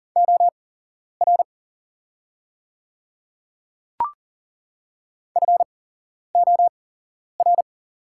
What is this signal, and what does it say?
Morse code, heard as a keyed tone, about 40 wpm, spaced as for 11 wpm, 700 Hz. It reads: OR FOR
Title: 8x Character Spacing - Top 100 Words - 40wpm